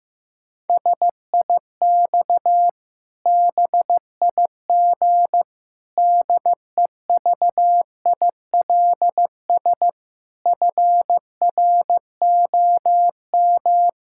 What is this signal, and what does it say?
Morse code, heard as a keyed tone, about 15 wpm, 700 Hz. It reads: SIX BIG DEVILS FROM
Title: Pangram Challenge v2 - 15wpm